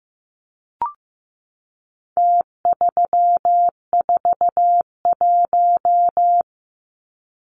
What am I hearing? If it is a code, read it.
Morse code, heard as a keyed tone, about 15 wpm, 700 Hz. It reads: T341
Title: Numbers - Contest Style - 4-Digit 15wpm